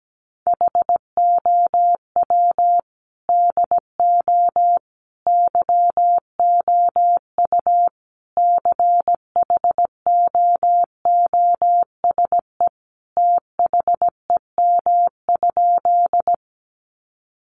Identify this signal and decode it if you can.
Morse code, heard as a keyed tone, about 17 wpm, 700 Hz. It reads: HOW DO YOU CHOOSE THEM?
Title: Sentences from Top 700 Words 17wpm